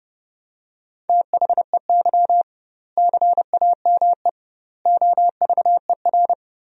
Morse code, heard as a keyed tone, about 32 wpm, 700 Hz. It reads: THEY CAME OVER
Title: 2x Word-Spacing - Sentences from Top 300 Words - 30wpm